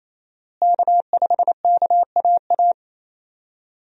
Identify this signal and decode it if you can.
Morse code, heard as a keyed tone, about 28 wpm, 700 Hz. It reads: K5KAA
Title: USA Call Signs - 5 Characters - Repeated 2x - 28wpm